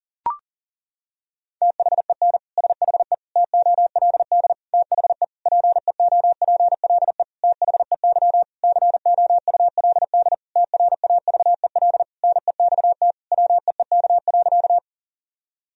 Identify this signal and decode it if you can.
Morse code, heard as a keyed tone, about 40 wpm, 700 Hz. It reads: THEN SHE TOLD THE PEOPLE THEY COULD TRAVEL NEXT WEEK.
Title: Sentences from Top 1000 Words - Review Part 2 - 40wpm